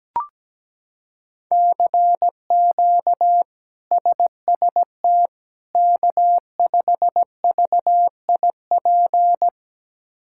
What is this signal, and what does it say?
Morse code, heard as a keyed tone, about 17 wpm, 700 Hz. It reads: CQ SST K5VIP
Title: K1USN SST QSO Practice - 17wpm